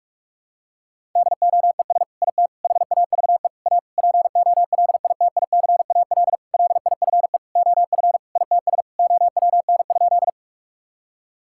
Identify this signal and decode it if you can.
Morse code, heard as a keyed tone, about 46 wpm, 700 Hz. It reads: DOES IT HAVE A POLITICAL LIFE OF ITS OWN?
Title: Sentences from Top 1000 Words - Review Part 5 - 45wpm